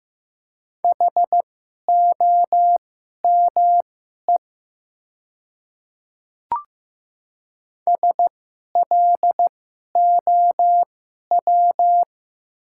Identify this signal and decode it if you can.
Morse code, heard as a keyed tone, about 15 wpm, spaced as for 11 wpm, 700 Hz. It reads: HOME SLOW
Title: 2x Character Spacing - Top 500 Words - 15wpm